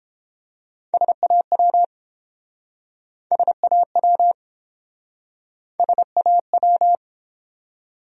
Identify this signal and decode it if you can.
Morse code, heard as a keyed tone, about 31 wpm, 700 Hz. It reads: SAW SAW SAW